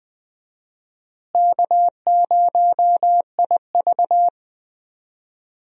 Morse code, heard as a keyed tone, about 20 wpm, 700 Hz. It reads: K0IV
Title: USA Call Signs - 4 Characters - Repeated 2x - 20wpm